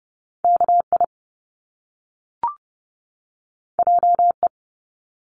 Morse code, heard as a keyed tone, about 30 wpm, 700 Hz. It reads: KI JE
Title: Top Two Letter Combinations in Words 30wpm